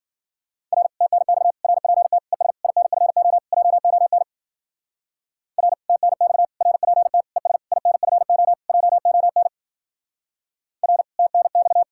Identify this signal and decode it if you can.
Morse code, heard as a keyed tone, about 47 wpm, 700 Hz. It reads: R TNX RPT ES INFO JON R TNX RPT ES INFO JON R TNX
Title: Speed-Racing Common QSO Phrases Encore - 40wpm